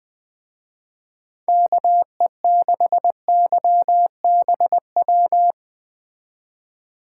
Morse code, heard as a keyed tone, about 20 wpm, 700 Hz. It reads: KE6YBW